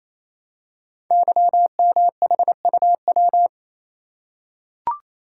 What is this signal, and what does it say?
Morse code, heard as a keyed tone, about 27 wpm, 700 Hz. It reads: YMHUW
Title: Code Groups - 5 Letter:Numbers - 28wpm